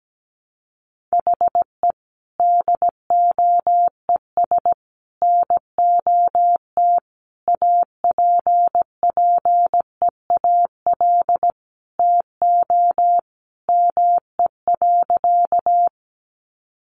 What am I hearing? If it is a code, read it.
Morse code, heard as a keyed tone, about 17 wpm, 700 Hz. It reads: HE DOES NOT APPEAL TO ME.
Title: Sentences from Top 1300 Words 17wpm